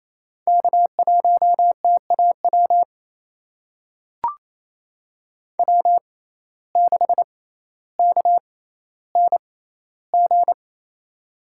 Morse code, heard as a keyed tone, about 28 wpm, spaced as for 14 wpm, 700 Hz. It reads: K1TAW W6KNG
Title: USA Call Signs - 5 Characters - 6x Character Spacing - 28wpm